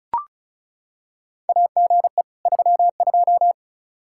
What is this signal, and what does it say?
Morse code, heard as a keyed tone, about 36 wpm, 700 Hz. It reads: AGE 32
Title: Practice QSO - Exchange 3 - More Info - 35wpm